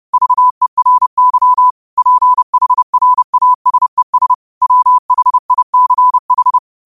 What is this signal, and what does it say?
Morse code, heard as a keyed tone, about 30 wpm, 1000 Hz. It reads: VERY PHRASES WHICH
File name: qotdfiles / 30wpm